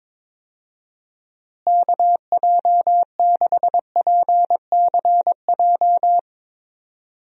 Morse code, heard as a keyed tone, about 22 wpm, 700 Hz. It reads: KJ6PCJ
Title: USA Call Signs - 6 Characters - Repeated 2x - 22wpm